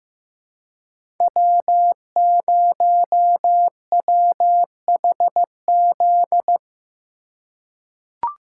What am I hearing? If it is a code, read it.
Morse code, heard as a keyed tone, about 15 wpm, 700 Hz. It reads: W0WHZ
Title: USA Call Signs - 5 Characters - 4x Character Spacing - 15wpm